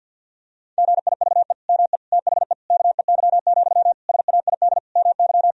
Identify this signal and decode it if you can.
Morse code, heard as a keyed tone, about 50 wpm, 700 Hz. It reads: GIVE ME THE KEY, SAID MY
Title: Sentences from Top 1000 Words - Review Part 4 - 50wpm